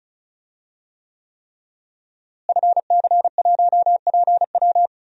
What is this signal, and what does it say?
Morse code, heard as a keyed tone, about 36 wpm, 700 Hz. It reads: FC1PW
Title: DXCC-100 Call Signs - 5 Characters - Repeated 3x - 35wpm